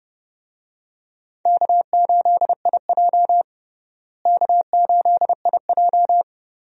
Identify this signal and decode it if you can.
Morse code, heard as a keyed tone, about 32 wpm, 700 Hz. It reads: K8IJ K8IJ
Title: USA Call Signs - 4 Characters - Repeated 3x - 30wpm